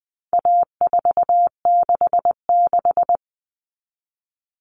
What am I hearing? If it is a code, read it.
Morse code, heard as a keyed tone, about 20 wpm, 700 Hz. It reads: A466
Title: Common Cut Numbers - 4-Digit 20wpm